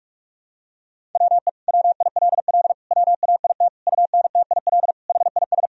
Morse code, heard as a keyed tone, about 41 wpm, 700 Hz. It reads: WE WILL WAIT UNTIL HIS
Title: Sentences from Top 1000 Words - Review Part 1 - 45wpm